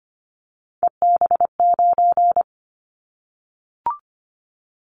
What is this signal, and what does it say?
Morse code, heard as a keyed tone, about 25 wpm, 700 Hz. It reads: EB9
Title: Callsign Prefixes for Top 100 DXCC - 25wpm